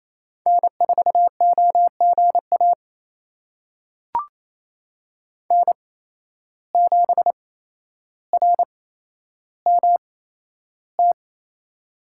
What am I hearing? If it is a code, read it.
Morse code, heard as a keyed tone, about 28 wpm, spaced as for 12 wpm, 700 Hz. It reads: N4OGA N7RMT
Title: USA Call Signs - 5 Characters - 8x Character Spacing - 28wpm